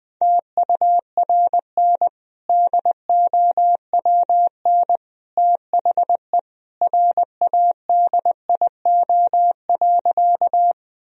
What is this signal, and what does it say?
Morse code, heard as a keyed tone, about 20 wpm, 700 Hz. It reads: TURN DOWN THE RADIO.
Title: Sentences from Top 1000 Words - Review Part 3 - 20wpm